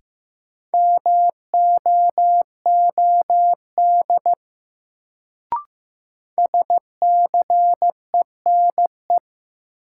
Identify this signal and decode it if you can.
Morse code, heard as a keyed tone, about 15 wpm, 700 Hz. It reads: MOOD SCENE
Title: Sentences from Top 1100 Words - New Word Drill - 15wpm